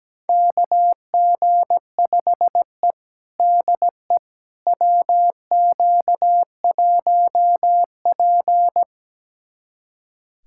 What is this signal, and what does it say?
Morse code, heard as a keyed tone, about 17 wpm, 700 Hz. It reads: KG5E DE WQ1P
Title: USA Callsign DE Callsign - 4 Characters - 17wpm